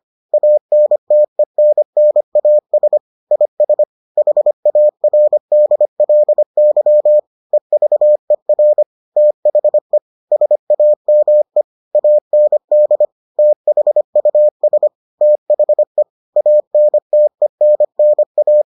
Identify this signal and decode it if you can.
Morse code, heard as a keyed tone, about 25 wpm, 600 Hz. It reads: ANTENNAS IS HARDLY EVER THE SAME AND THUS THE ANTENNA